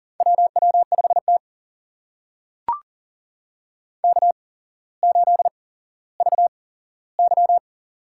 Morse code, heard as a keyed tone, about 39 wpm, spaced as for 17 wpm, 700 Hz. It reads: WW5T K8VY
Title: USA Call Signs - 4 Characters - 8x Character Spacing - 40wpm